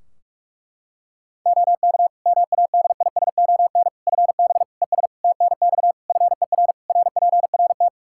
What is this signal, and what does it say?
Morse code, heard as a keyed tone, about 46 wpm, 700 Hz. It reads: OK MADISON FB ES TNX FER RPRT
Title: Practice QSO - Exchange 2 - Info - 45wpm